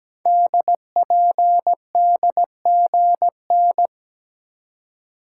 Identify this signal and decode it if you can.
Morse code, heard as a keyed tone, about 17 wpm, 700 Hz. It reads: DPDGN
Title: Code Groups - 5 Letters - 17wpm